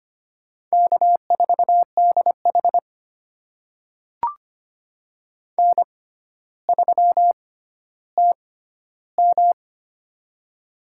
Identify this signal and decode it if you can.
Morse code, heard as a keyed tone, about 25 wpm, spaced as for 13 wpm, 700 Hz. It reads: K4DH N3TM
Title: USA Call Signs - 4 Characters - 6x Character Spacing - 25wpm